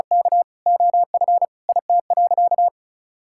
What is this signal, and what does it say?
Morse code, heard as a keyed tone, about 36 wpm, 700 Hz. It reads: K OF IT.
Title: Sentences from Top 1000 Words - Review Part 5 - 35wpm